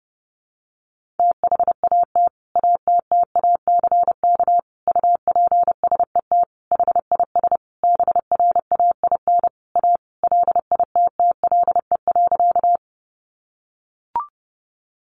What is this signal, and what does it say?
Morse code, heard as a keyed tone, about 30 wpm, 700 Hz. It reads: THAT ATTACK UPSET HIS BRAIN A LITTLE.
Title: Sentences from Top 1500 Words 30wpm